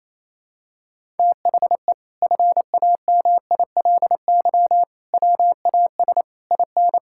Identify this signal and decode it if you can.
Morse code, heard as a keyed tone, about 28 wpm, 700 Hz. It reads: THE FAMILY WAS IN
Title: Sentences from Top 1000 Words - Review Part 6 - 28wpm